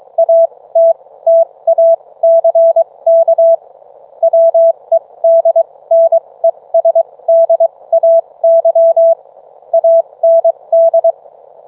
Morse code, heard as a keyed tone, about 23 wpm, spaced as for 18 wpm, 660 Hz. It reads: ATTACK WEDNESDAY AND